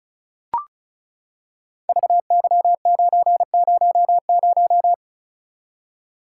Morse code, heard as a keyed tone, about 36 wpm, 690 Hz. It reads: VY900
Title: Code Groups - 5 Letter:Numbers - 35wpm